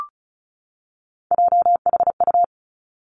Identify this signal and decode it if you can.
Morse code, heard as a keyed tone, about 36 wpm, 700 Hz. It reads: JHU
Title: Three Challenging Letter-Number Combinations 35wpm